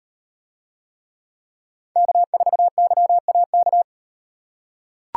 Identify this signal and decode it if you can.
Morse code, heard as a keyed tone, about 38 wpm, 700 Hz. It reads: K4YAK